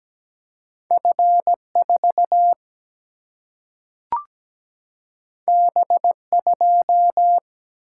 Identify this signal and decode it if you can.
Morse code, heard as a keyed tone, about 17 wpm, 700 Hz. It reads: F 4 B 2
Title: Single-Letter Single-Number 17wpm